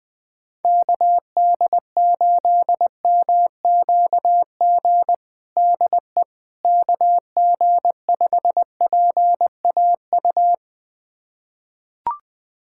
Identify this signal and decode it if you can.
Morse code, heard as a keyed tone, about 20 wpm, 700 Hz. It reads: KD8MQG DE KG5PAU
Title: USA Callsign DE Callsign 20wpm